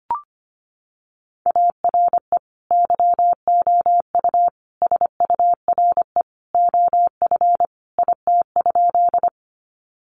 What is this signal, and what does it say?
Morse code, heard as a keyed tone, about 25 wpm, 700 Hz. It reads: ARE YOU SURE OF IT?
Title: Sentences from Top 300 Words 25wpm